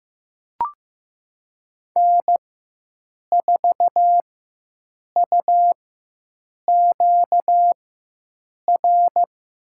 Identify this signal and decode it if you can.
Morse code, heard as a keyed tone, about 15 wpm, spaced as for 7 wpm, 700 Hz. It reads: N4UQR
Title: USA Call Signs - 4x Character Spacing - 15wpm